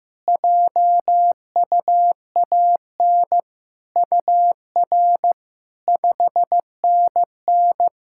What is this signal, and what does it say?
Morse code, heard as a keyed tone, about 15 wpm, 700 Hz. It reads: JUAN UR 5NN
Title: SOTA QSO Practice 15wpm